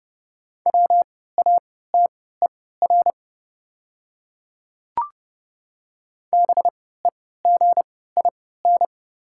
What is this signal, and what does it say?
Morse code, heard as a keyed tone, about 30 wpm, spaced as for 17 wpm, 690 Hz. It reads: WATER BEGIN
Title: 3x Character Spacing - Top 300 Words - 30wpm